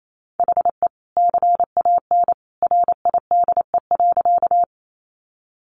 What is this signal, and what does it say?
Morse code, heard as a keyed tone, about 28 wpm, 700 Hz. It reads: HE CAN RIDE.
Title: Sentences from Top 1200 Words 28wpm